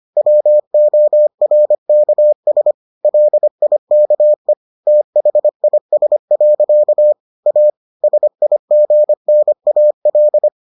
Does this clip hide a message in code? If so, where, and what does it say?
Yes: Morse code from 0.2 to 10.5 s, heard as a keyed tone, about 25 wpm, 600 Hz. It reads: WORKS LIKE THIS. A SIGNAL